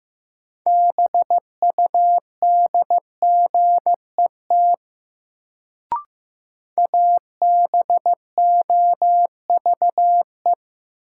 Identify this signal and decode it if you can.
Morse code, heard as a keyed tone, about 15 wpm, 700 Hz. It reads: BUDGET ABOVE
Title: Sentences from Top 1200 Words - New Word Drill Alt - 15wpm